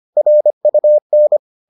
Morse code, heard as a keyed tone, about 25 wpm, 600 Hz. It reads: RUN